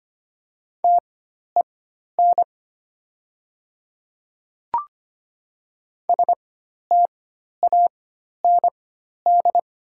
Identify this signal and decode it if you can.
Morse code, heard as a keyed tone, about 25 wpm, spaced as for 12 wpm, 700 Hz. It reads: TEN STAND
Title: 4x Character Spacing - Top 500 Words - 25wpm